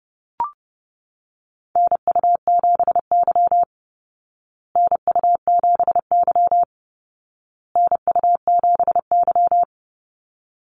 Morse code, heard as a keyed tone, about 30 wpm, 700 Hz. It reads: NU7Y NU7Y NU7Y
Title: Call Signs Repeated 2x - 30wpm